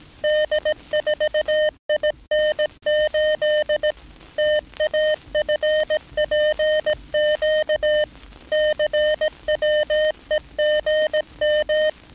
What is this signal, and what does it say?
Morse code, heard as a keyed tone, about 18 wpm, 610 Hz. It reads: D4IN8 TAFPQ CWEGM
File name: training / NAVPERS 11049to11060 / NAVPERS 11059 RA RB